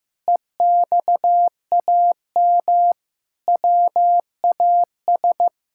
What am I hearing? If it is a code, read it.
Morse code, heard as a keyed tone, about 15 wpm, 700 Hz. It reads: EXAM WAS
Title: Sentences from Top 1000 Words - Review Part 5 - 15wpm